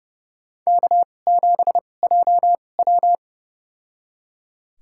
Morse code, heard as a keyed tone, about 30 wpm, spaced as for 22 wpm, 700 Hz. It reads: K7JW
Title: USA Call Signs - 2x Character Spacing - 30wpm